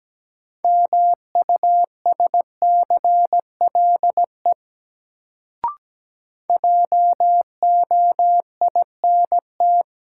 Morse code, heard as a keyed tone, about 17 wpm, 700 Hz. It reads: MUSCLE JOINT
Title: Sentences from Top 1300 Words - New Word Drill - 17wpm